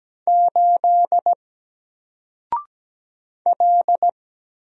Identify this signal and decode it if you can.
Morse code, heard as a keyed tone, about 17 wpm, 700 Hz. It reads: 8 L
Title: Single Letter-Number 17wpm